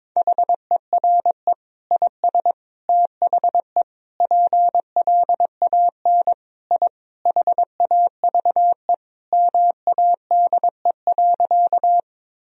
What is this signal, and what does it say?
Morse code, heard as a keyed tone, about 22 wpm, 700 Hz. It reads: HERE IS THE PLAN I HAVE MADE.
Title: Sentences from Top 1000 Words - Review Part 2 - 22wpm